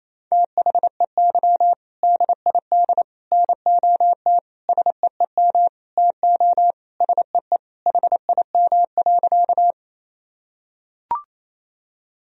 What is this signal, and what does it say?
Morse code, heard as a keyed tone, about 28 wpm, 700 Hz. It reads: THEY DID NOT SEEM TO SEE HIM.
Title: Sentences from Top 1000 Words - Review Part 1 - 28wpm